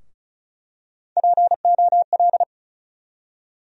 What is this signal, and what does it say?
Morse code, heard as a keyed tone, about 36 wpm, 700 Hz. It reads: POL
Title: USA Call Sign Postfixes - 3 Letters 35wpm